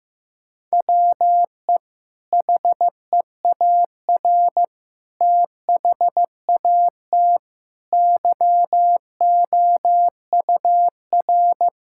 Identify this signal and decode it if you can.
Morse code, heard as a keyed tone, about 15 wpm, 700 Hz. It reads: WE HEAR THAT YOUR
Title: Sentences from Top 1000 Words - Review Part 5 - 15wpm